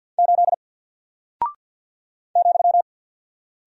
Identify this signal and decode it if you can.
Morse code, heard as a keyed tone, about 49 wpm, 700 Hz. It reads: 8 ,
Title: Single Character (Letter:Number:Punctuation:Prosign) 50wpm